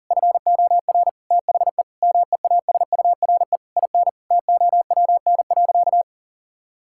Morse code, heard as a keyed tone, about 40 wpm, 700 Hz. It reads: FOR THE MEASURE IN TOWN.
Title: Sentences from Top 1000 Words - Review Part 5 - 40wpm